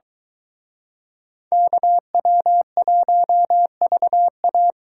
Morse code, heard as a keyed tone, about 22 wpm, 700 Hz. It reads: KW1VA